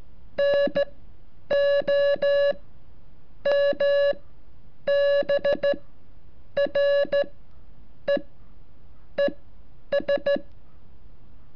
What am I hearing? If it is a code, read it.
Morse code, heard as a keyed tone, about 12 wpm, spaced as for 7 wpm, 570 Hz. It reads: NOMBRE ES
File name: curso / ejercicio 34